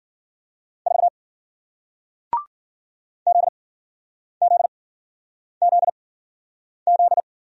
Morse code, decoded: 4 ZZZZ

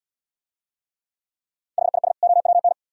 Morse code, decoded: HIS OWN

